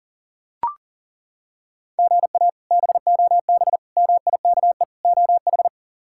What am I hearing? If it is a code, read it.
Morse code, heard as a keyed tone, about 37 wpm, 700 Hz. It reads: GA BOB MIKE OH
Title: K1USN SST QSO Practice - 40wpm